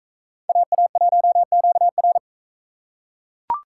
Morse code, heard as a keyed tone, about 42 wpm, 700 Hz. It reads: AA1QR